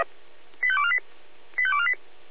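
Five-tone selective call (ZVEI), heard as quick runs of five tones, tones of about 70 ms. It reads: 85248 85248
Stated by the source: ZVEI five-tone selective call received by SDR.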